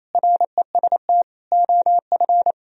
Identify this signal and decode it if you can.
Morse code, heard as a keyed tone, about 36 wpm, 700 Hz. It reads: REST OF